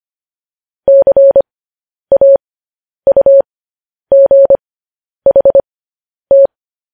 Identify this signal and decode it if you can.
Morse code, heard as a keyed tone, about 25 wpm, spaced as for 10 wpm, 550 Hz. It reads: CAUGHT
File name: podcast / News headlines-1x-25x10